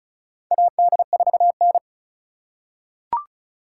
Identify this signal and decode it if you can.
Morse code, heard as a keyed tone, about 36 wpm, 690 Hz. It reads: AD4N
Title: USA Call Signs - 4 Characters - 2x Character Spacing - 35wpm